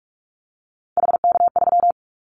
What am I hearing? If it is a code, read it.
Morse code, heard as a keyed tone, about 46 wpm, 700 Hz. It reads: HK3
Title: Callsign Prefixes for Top 100 DXCC - 45wpm